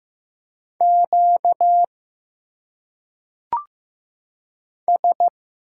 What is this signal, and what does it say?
Morse code, heard as a keyed tone, about 15 wpm, 700 Hz. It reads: Q S
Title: Single Letters - Phonetic Pronunciation 15wpm